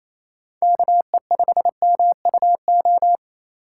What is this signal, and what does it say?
Morse code, heard as a keyed tone, about 28 wpm, 700 Hz. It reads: KE5MUO
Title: USA Call Signs - 6 Characters - 4x Character Spacing - 28wpm